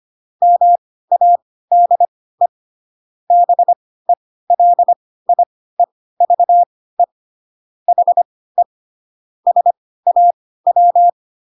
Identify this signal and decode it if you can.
Morse code, heard as a keyed tone, about 25 wpm, spaced as for 16 wpm, 700 Hz. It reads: MADE BELIEVE HE SAW
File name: Pinocchio / Pinocchio ch3 10 25 ik1hge